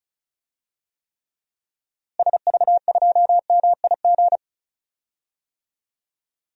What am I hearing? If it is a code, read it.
Morse code, heard as a keyed tone, about 36 wpm, 700 Hz. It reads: SV2MIG
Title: DXCC-100 Call Signs - 6 Characters - Repeated 3x - 35wpm